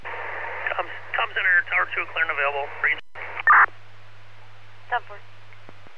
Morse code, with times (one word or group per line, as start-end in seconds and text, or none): none